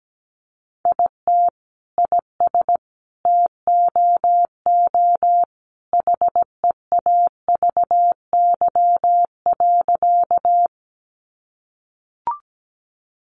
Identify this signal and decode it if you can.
Morse code, heard as a keyed tone, about 17 wpm, 700 Hz. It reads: IT IS TOO HEAVY.
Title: Sentences from Top 1100 Words 17wpm